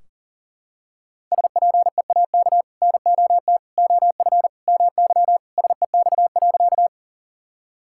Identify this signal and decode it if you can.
Morse code, heard as a keyed tone, about 40 wpm, 700 Hz. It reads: SPEAK NOT OF MY SEX.
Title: Sentences from Top 1600 Words 40wpm